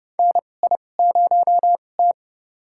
Morse code, heard as a keyed tone, about 30 wpm, spaced as for 22 wpm, 700 Hz. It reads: NI0T
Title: USA Call Signs - 4 Characters - 2x Character Spacing - 30wpm